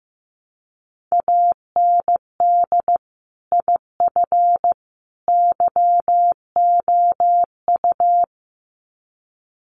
Morse code, heard as a keyed tone, about 15 wpm, 700 Hz. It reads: AND IF YOU
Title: Sets of 3 Words 15wpm